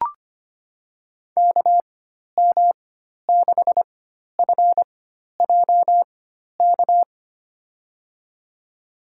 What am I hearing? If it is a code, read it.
Morse code, heard as a keyed tone, about 25 wpm, spaced as for 12 wpm, 700 Hz. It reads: KM6FJK
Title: USA Call Signs - 6 Characters - 4x Character Spacing - 25wpm